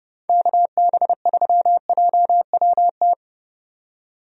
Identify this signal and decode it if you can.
Morse code, heard as a keyed tone, about 30 wpm, 700 Hz. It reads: KB3JWT
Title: USA Call Signs - 6 Characters - 30wpm